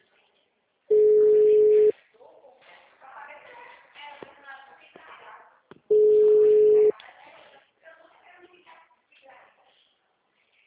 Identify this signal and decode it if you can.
Ringback tone (Europe, 425 Hz), with speech.